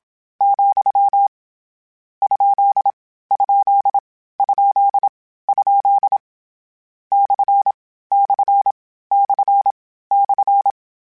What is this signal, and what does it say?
Morse code, heard as a keyed tone, about 27 wpm, 800 Hz. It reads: , ? ? ? ? / / / /
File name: Misc / Misc-mp3s / Punct @25